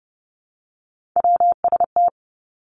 Morse code, heard as a keyed tone, about 30 wpm, 700 Hz. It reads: WST
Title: Three Challenging Letter-Number Combinations 30wpm